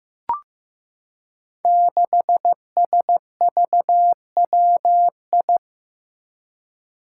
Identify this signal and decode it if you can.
Morse code, heard as a keyed tone, about 15 wpm, 700 Hz. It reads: 6SVWI
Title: Five Challenging Letter-Number Combinations 15wpm